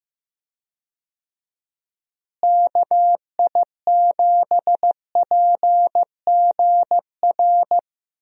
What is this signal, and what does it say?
Morse code, heard as a keyed tone, about 15 wpm, 700 Hz. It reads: KI7PGR
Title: USA Call Signs - 6 Characters - Repeated 3x - 15wpm